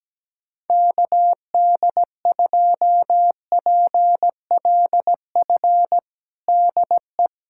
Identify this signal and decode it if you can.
Morse code, heard as a keyed tone, about 17 wpm, 700 Hz. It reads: KD2PLF DE
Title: USA Callsign DE Callsign - 6 Characters - 17wpm